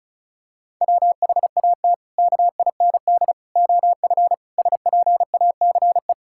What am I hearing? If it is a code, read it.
Morse code, heard as a keyed tone, about 34 wpm, 700 Hz. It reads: WHAT KIND OF SPACE
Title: Sentences from Top 1000 Words - Review Part 5 - 35wpm